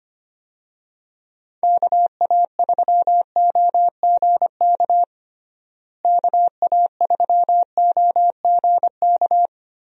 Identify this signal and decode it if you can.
Morse code, heard as a keyed tone, about 25 wpm, 700 Hz. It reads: KA3OGK KA3OGK
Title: USA Call Signs - 6 Characters - Repeated 3x - 25wpm